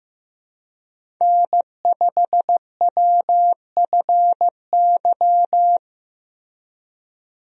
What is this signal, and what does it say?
Morse code, heard as a keyed tone, about 15 wpm, 700 Hz. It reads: N5WFY
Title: USA Call Signs - 5 Characters - Repeated 3x - 15wpm